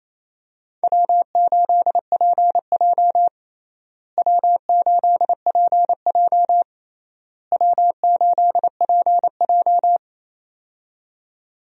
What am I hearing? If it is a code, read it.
Morse code, heard as a keyed tone, about 28 wpm, 700 Hz. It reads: W8PJ W8PJ W8PJ
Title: USA Call Signs - 4 Characters - Repeated 3x - 28wpm